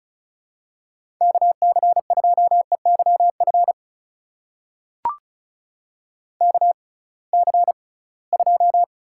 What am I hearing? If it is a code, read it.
Morse code, heard as a keyed tone, about 36 wpm, spaced as for 23 wpm, 700 Hz. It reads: KC2EYF KC2
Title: USA Call Signs - 6 Characters - 6x Character Spacing - 35wpm